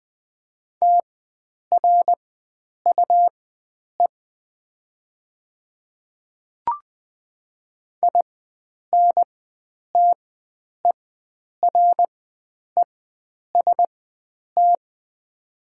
Morse code, heard as a keyed tone, about 20 wpm, spaced as for 9 wpm, 700 Hz. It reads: TRUE INTEREST
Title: 4x Character Spacing - Top 500 Words - 20wpm